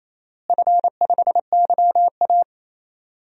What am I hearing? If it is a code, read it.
Morse code, heard as a keyed tone, about 27 wpm, 700 Hz. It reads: F5YA
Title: DXCC-100 Call Signs - 4 Characters - Repeated 3x - 28wpm